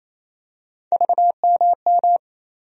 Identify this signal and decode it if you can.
Morse code, heard as a keyed tone, about 28 wpm, 700 Hz. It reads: VMM
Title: USA Call Sign Postfixes - 3 Letters 28wpm